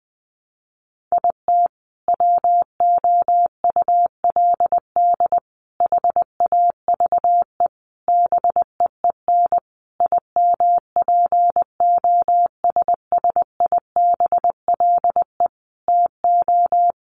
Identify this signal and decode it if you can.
Morse code, heard as a keyed tone, about 19 wpm, 700 Hz. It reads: IT WOULD HAVE BEEN IMPOSSIBLE TO
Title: Sentences from Top 1100 Words 20wpm